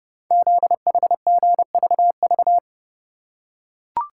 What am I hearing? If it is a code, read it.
Morse code, heard as a keyed tone, about 30 wpm, 700 Hz. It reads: ZHGVV
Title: Code Groups - 5 Letters - 30wpm